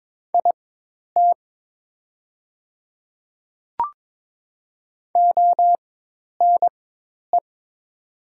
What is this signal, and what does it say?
Morse code, heard as a keyed tone, about 22 wpm, spaced as for 10 wpm, 700 Hz. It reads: IT ONE
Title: 4x Character Spacing - Top 200 Words - 22wpm